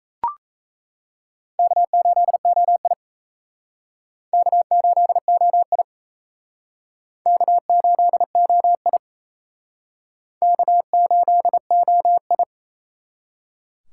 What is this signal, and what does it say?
Morse code, heard as a keyed tone, about 38 wpm, 700 Hz. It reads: K8OI K8OI K8OI K8OI